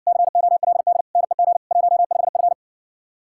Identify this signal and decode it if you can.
Morse code, heard as a keyed tone, about 60 wpm, 700 Hz. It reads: COLD NEG 15F